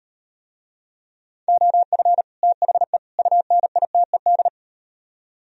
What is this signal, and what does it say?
Morse code, heard as a keyed tone, about 35 wpm, 700 Hz. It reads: OF THE UNITED